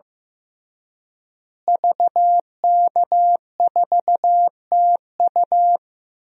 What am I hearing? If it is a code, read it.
Morse code, heard as a keyed tone, about 15 wpm, 700 Hz. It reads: VK4TU
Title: DXCC-100 Call Signs - 5 Characters - Repeated 3x - 15wpm